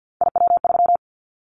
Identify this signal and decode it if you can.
Morse code, heard as a keyed tone, about 50 wpm, 700 Hz. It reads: IW3
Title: Callsign Prefixes for Top 50 DXCC - 50wpm